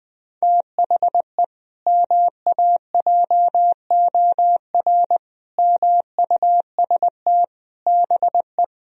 Morse code, heard as a keyed tone, about 20 wpm, 700 Hz. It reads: THE MAJOR MUST BE
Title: Sentences from Top 1000 Words - Review Part 4 - 20wpm